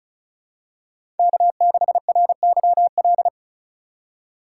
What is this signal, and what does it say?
Morse code, heard as a keyed tone, about 36 wpm, 700 Hz. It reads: K6RYL
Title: USA Call Signs - 5 Characters - 6x Character Spacing - 35wpm